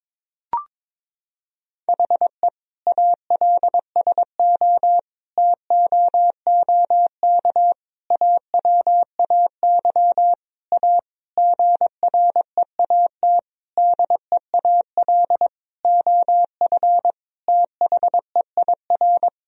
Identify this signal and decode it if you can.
Morse code, heard as a keyed tone, about 22 wpm, 700 Hz. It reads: HE ALSO TOOK AWAY A GREAT DEAL OF THEIR
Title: Sentences from Top 1000 Words - Review Part 1 - 22wpm